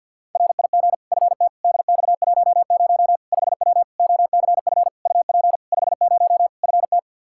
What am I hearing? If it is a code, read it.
Morse code, heard as a keyed tone, about 48 wpm, 700 Hz. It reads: RIG FT DX10 5W OCF UP 50 FT
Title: Practice QSO - Exchange 2 - Info - 50wpm